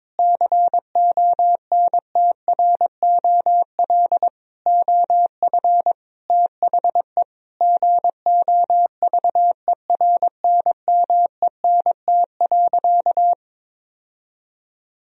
Morse code, heard as a keyed tone, about 22 wpm, 700 Hz. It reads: CONTROL OF THE GOVERNMENT.
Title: Sentences from Top 1000 Words - Review Part 3 - 22wpm